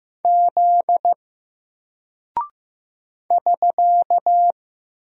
Morse code, heard as a keyed tone, about 15 wpm, 700 Hz. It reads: Z <SK>
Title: Single Character (Letter:Number:Punctuation:Prosign) 15wpm